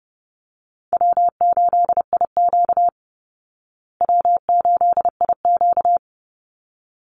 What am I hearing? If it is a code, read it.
Morse code, heard as a keyed tone, about 30 wpm, 700 Hz. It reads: W8IQ W8IQ
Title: Call Signs Repeated 3x - 30wpm